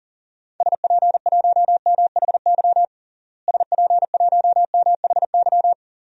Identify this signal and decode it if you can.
Morse code, heard as a keyed tone, about 37 wpm, 700 Hz. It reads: SP1MHY SP1MHY
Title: DXCC-100 Call Signs - 6 Characters - Repeated 3x - 40wpm